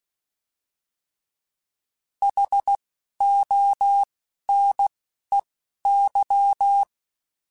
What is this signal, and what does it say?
Morse code, heard as a keyed tone, about 16 wpm, spaced as for 11 wpm, 800 Hz. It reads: HONEY